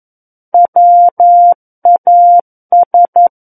Morse code, heard as a keyed tone, about 11 wpm, 700 Hz. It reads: WAS
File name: 11 WPM